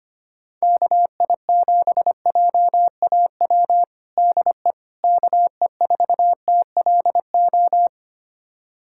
Morse code, heard as a keyed tone, about 25 wpm, 700 Hz. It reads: KI7JAW DE KE4TLO
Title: USA Callsign DE Callsign 25wpm